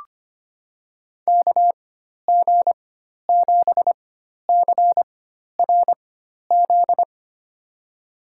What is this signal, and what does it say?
Morse code, heard as a keyed tone, about 25 wpm, spaced as for 12 wpm, 690 Hz. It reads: KG7CRZ
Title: USA Call Signs - 4x Character Spacing - 25wpm